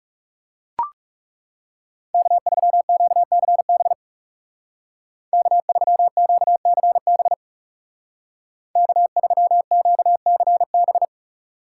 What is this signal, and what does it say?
Morse code, heard as a keyed tone, about 36 wpm, 700 Hz. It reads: K3QCB K3QCB K3QCB
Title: Speed Racing - USA Call Signs 30wpm